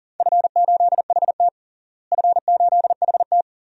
Morse code, heard as a keyed tone, about 40 wpm, 700 Hz. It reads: F8HT F8HT
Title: DXCC-100 Call Signs - 4 Characters - Repeated 2x - 40wpm